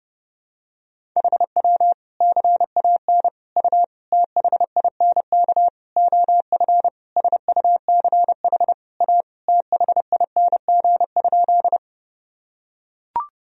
Morse code, heard as a keyed tone, about 30 wpm, 700 Hz. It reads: HW CAN U THINK OF SUCH A THING?
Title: Sentences from Top 200 Words and Abbreviations 30wpm